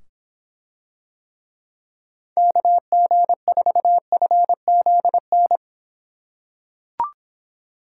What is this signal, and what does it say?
Morse code, heard as a keyed tone, about 25 wpm, 700 Hz. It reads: KG4FZN